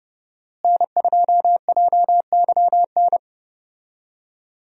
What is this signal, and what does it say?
Morse code, heard as a keyed tone, about 30 wpm, 700 Hz. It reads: N2JYN
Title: USA Call Signs - 5 Characters - Repeated 3x - 30wpm